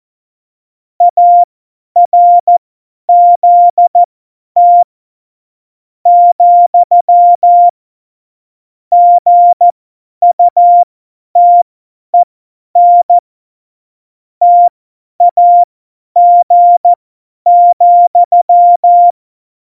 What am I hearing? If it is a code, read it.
Morse code, heard as a keyed tone, about 14 wpm, spaced as for 10 wpm, 700 Hz. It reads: ARZT , GUTEN TAG,